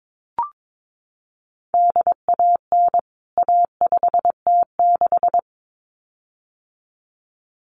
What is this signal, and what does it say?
Morse code, heard as a keyed tone, about 22 wpm, 700 Hz. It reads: DAN A5T6
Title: CWT Exchange 22wpm